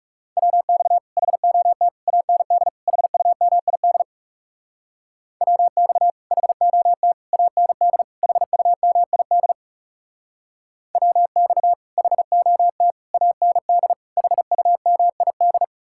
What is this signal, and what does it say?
Morse code, heard as a keyed tone, about 37 wpm, 700 Hz. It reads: WX HOT AND HUMID WX HOT AND HUMID WX HOT AND HUMID
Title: Speed-Racing Common QSO Phrases - 30wpm